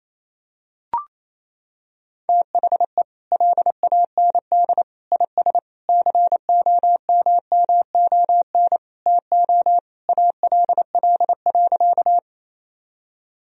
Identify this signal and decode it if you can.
Morse code, heard as a keyed tone, about 28 wpm, 700 Hz. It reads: THE LAND IS COMMON TO ALL.
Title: Sentences from Top 1000 Words - Review Part 3 - 28wpm